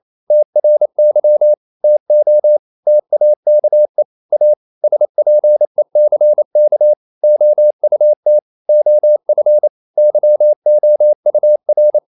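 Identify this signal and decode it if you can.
Morse code, heard as a keyed tone, about 28 wpm, 600 Hz. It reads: TRY TO TAKE A SPECK OUT OF YOUR
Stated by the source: Morse code practice at 28 wpm, text of Ken Follett's novel Never in English, from CW med Gnister.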